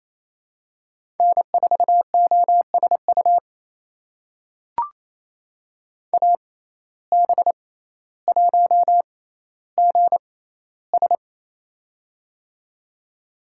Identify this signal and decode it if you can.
Morse code, heard as a keyed tone, about 28 wpm, spaced as for 14 wpm, 700 Hz. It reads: N4OSU AB1GS
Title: USA Call Signs - 5 Characters - 6x Character Spacing - 28wpm